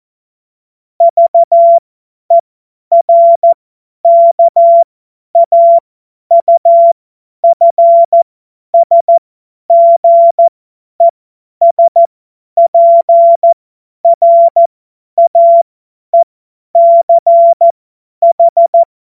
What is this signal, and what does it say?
Morse code, heard as a keyed tone, about 14 wpm, spaced as for 10 wpm, 680 Hz. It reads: VERKAUFSGESPRAECH
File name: Morsekurs / Witze / Witz 16 Januar 2024 50 F70